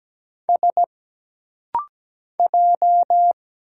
Morse code, heard as a keyed tone, about 17 wpm, 700 Hz. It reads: S J
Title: Single Letter-Number - Rapid-Fire 17wpm